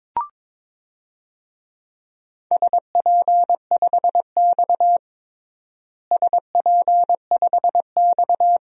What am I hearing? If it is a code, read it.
Morse code, heard as a keyed tone, about 21 wpm, 700 Hz. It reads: SP5X SP5X
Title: DXCC-100 Call Signs - 4 Characters - Repeated 2x - 22wpm